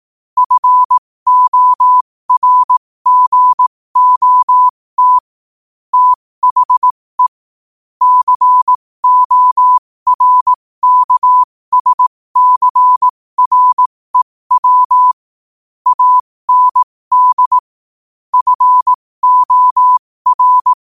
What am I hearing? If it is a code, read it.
Morse code, heard as a keyed tone, about 18 wpm, spaced as for 15 wpm, 1000 Hz. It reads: FORGOT THE CORKSCREW AND FOR
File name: qotdfiles / 15wpm